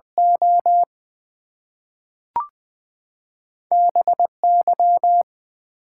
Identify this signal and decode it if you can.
Morse code, heard as a keyed tone, about 20 wpm, 700 Hz. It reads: O BY
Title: Two Letter Words - 20wpm